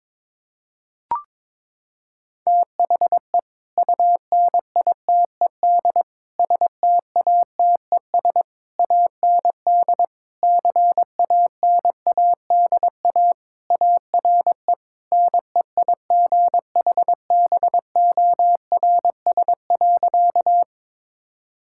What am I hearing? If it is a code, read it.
Morse code, heard as a keyed tone, about 22 wpm, 700 Hz. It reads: THE UNITED STATES AND CANADA ARE NEIGHBORS.
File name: practice / 22wpm / Sentences with 3-Word Phrases - 22wpm